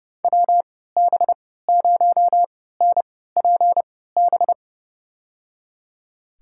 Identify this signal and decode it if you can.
Morse code, heard as a keyed tone, about 32 wpm, spaced as for 17 wpm, 700 Hz. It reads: WB0NPB
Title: USA Call Signs - 6 Characters - 3x Character Spacing - 30wpm